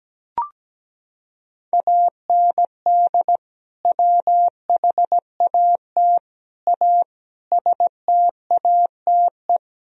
Morse code, heard as a keyed tone, about 17 wpm, 700 Hz. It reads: AND WHAT A STATE
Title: Sentences from Top 1000 Words - Review Part 2 - 17wpm